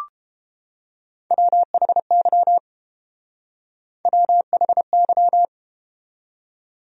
Morse code, heard as a keyed tone, about 32 wpm, 700 Hz. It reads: WHY WHY